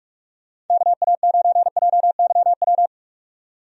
Morse code, heard as a keyed tone, about 41 wpm, 700 Hz. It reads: KA9JYW
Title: USA Call Signs - 6 Characters - 45wpm